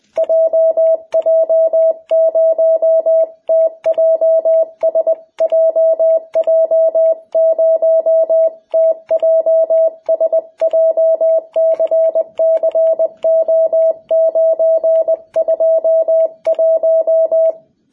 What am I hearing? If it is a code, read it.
Morse code, heard as a keyed tone, about 20 wpm, 660 Hz. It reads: JJ0TJS JJ0TJS JCCO921
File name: jh8cbh / 2021cw